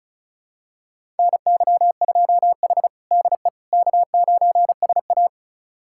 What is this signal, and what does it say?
Morse code, heard as a keyed tone, about 36 wpm, 700 Hz. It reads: NY2H DE K9SA
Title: USA Callsign DE Callsign - 4 Characters - 35wpm